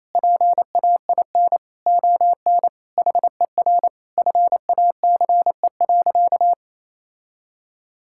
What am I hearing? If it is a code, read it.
Morse code, heard as a keyed tone, about 28 wpm, 700 Hz. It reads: PAIN ON HER FACE.
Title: Sentences from Top 1600 Words 28wpm